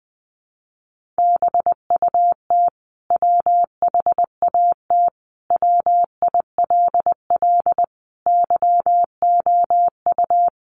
Morse code, heard as a keyed tone, about 20 wpm, 700 Hz. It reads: BUT WHAT WILL YOU
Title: Sentences from Top 100 Words 20wpm